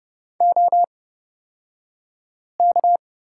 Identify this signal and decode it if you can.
Morse code, heard as a keyed tone, about 31 wpm, spaced as for 5 wpm, 700 Hz. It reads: OK
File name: Lesson 38d - Letter q - Two Characters 30wpm